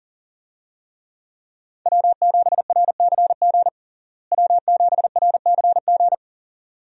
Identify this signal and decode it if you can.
Morse code, heard as a keyed tone, about 37 wpm, 700 Hz. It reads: W7RCG W7RCG
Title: USA Call Signs - 5 Characters - Repeated 2x - 40wpm